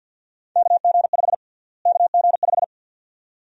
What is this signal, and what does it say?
Morse code, heard as a keyed tone, about 52 wpm, 700 Hz. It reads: KG5 KG5
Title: USA Call Sign Prefixes - Repeated 2x - 50wpm